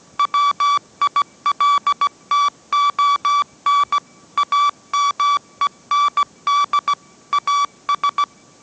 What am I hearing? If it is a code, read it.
Morse code, heard as a keyed tone, about 17 wpm, 1150 Hz. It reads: WILTON AMEND AS
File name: mrx / mrx080 18wpm